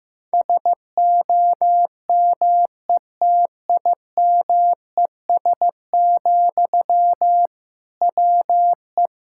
Morse code, heard as a keyed tone, about 15 wpm, 700 Hz. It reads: SOMETIMES, WE
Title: Sentences from Top 1000 Words - Review Part 5 - 15wpm